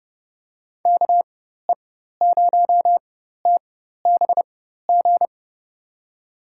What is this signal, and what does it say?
Morse code, heard as a keyed tone, about 31 wpm, spaced as for 14 wpm, 700 Hz. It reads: KE0TBG
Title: USA Call Signs - 6 Characters - 4x Character Spacing - 30wpm